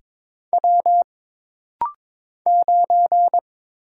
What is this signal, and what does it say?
Morse code, heard as a keyed tone, about 23 wpm, 700 Hz. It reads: W 9
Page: This practice set is composed of Letters, Numbers, common punctuations, and common prosigns.